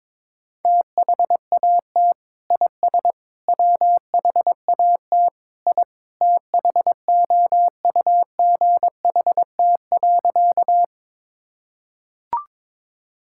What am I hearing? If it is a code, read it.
Morse code, heard as a keyed tone, about 22 wpm, 700 Hz. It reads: THAT IS WHAT I THOUGHT.
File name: practice / 22wpm / Sentences with 2-Word Phrases - 22wpm